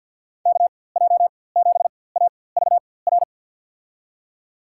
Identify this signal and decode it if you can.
Morse code, heard as a keyed tone, about 47 wpm, spaced as for 23 wpm, 700 Hz. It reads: KJ7AVR
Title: USA Call Signs - 6 Characters - 4x Character Spacing - 50wpm